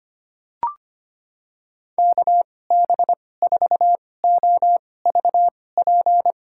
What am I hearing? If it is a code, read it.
Morse code, heard as a keyed tone, about 25 wpm, spaced as for 18 wpm, 700 Hz. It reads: KB4OVP
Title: USA Call Signs - 6 Characters - 2x Character Spacing - 25wpm